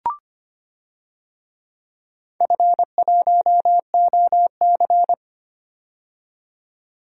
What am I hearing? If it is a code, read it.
Morse code, heard as a keyed tone, about 25 wpm, 700 Hz. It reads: F1OC
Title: DXCC-100 Call Signs - 4 Characters - 25wpm